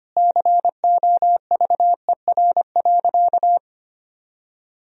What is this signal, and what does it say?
Morse code, heard as a keyed tone, about 25 wpm, 700 Hz. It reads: COVER.
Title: Sentences from Top 1000 Words - Review Part 1 - 25wpm